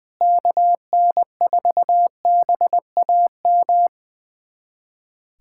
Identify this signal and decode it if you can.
Morse code, heard as a keyed tone, about 20 wpm, 700 Hz. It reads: KN4BAM
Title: USA Callsign DE Callsign 20wpm